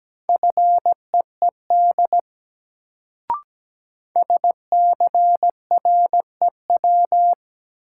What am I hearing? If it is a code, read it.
Morse code, heard as a keyed tone, about 17 wpm, 700 Hz. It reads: FEED SCREW
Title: Sentences from Top 1600 Words - New Word Drill - 17wpm